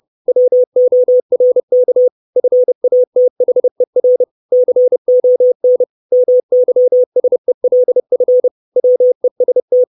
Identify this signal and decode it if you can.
Morse code, heard as a keyed tone, about 30 wpm, 500 Hz. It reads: WORK FATHER CON MYSELF WEST
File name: 632 Words 30WPM